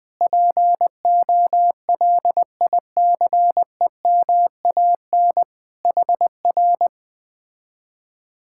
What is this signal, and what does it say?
Morse code, heard as a keyed tone, about 20 wpm, 700 Hz. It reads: POLICEMAN HR
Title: Speed-Racing Common QSO Phrases Encore - 17wpm